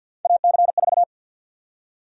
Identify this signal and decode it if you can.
Morse code, heard as a keyed tone, about 53 wpm, 700 Hz. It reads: AC4
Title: USA Call Sign Prefixes - 50wpm